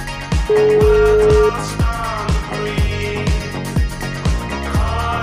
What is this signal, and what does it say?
Ringback tone (Europe, 425 Hz), with noise.